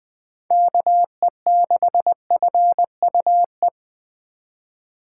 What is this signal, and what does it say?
Morse code, heard as a keyed tone, about 20 wpm, 700 Hz. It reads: KE6FUE
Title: USA Call Signs - 6 Characters - 8x Character Spacing - 20wpm